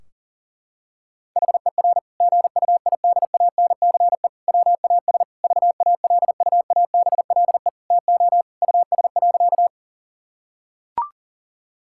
Morse code, heard as a keyed tone, about 40 wpm, 700 Hz. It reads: HER GUIDANCE WAS VALUABLE TO US.
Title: Sentences from Top 1600 Words 40wpm